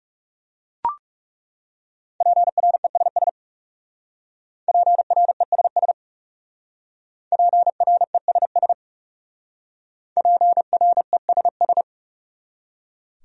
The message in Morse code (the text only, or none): PRESS PRESS PRESS PRESS